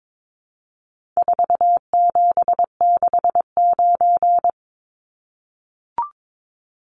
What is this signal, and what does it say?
Morse code, heard as a keyed tone, about 22 wpm, 700 Hz. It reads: 4769
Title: Numbers - 4-Digit 22wpm